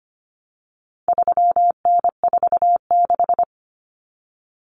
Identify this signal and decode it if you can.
Morse code, heard as a keyed tone, about 25 wpm, 700 Hz. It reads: 3N46
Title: Common Cut Numbers - 4-Digit 25wpm